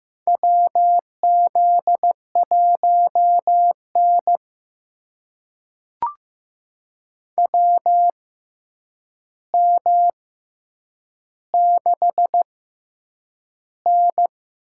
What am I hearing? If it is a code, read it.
Morse code, heard as a keyed tone, about 15 wpm, spaced as for 8 wpm, 700 Hz. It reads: WZ1N WM6N
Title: USA Call Signs - 4 Characters - 6x Character Spacing - 15wpm